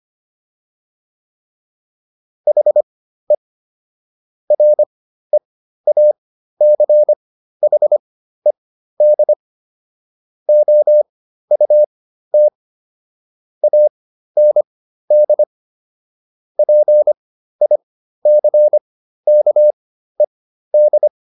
Morse code, heard as a keyed tone, about 25 wpm, spaced as for 13 wpm, 600 Hz. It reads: HE REACHED OUT AND PICKED